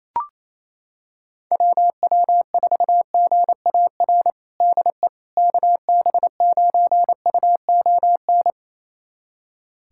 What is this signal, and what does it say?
Morse code, heard as a keyed tone, about 28 wpm, 700 Hz. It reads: WW4GAR DE KB9UON
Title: USA Callsign DE Callsign - 6 Characters - 28wpm